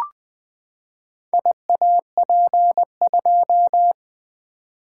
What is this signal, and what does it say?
Morse code, heard as a keyed tone, about 20 wpm, 700 Hz. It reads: IAP2
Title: Four Letter-Number Combinations - 20wpm